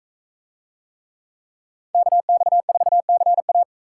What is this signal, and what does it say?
Morse code, heard as a keyed tone, about 44 wpm, 700 Hz. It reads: KX4CA